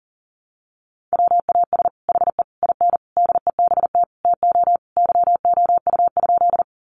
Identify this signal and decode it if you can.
Morse code, heard as a keyed tone, about 40 wpm, 700 Hz. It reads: WAS HE IN DEBT TO YOU?
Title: Sentences from Top 1300 Words 40wpm